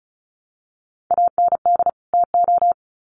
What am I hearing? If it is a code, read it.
Morse code, heard as a keyed tone, about 34 wpm, 700 Hz. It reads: AND TO
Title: Sets of 2 Words 35wpm